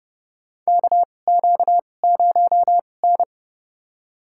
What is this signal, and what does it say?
Morse code, heard as a keyed tone, about 32 wpm, spaced as for 22 wpm, 700 Hz. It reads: KQ0N
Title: USA Call Signs - 4 Characters - 2x Character Spacing - 30wpm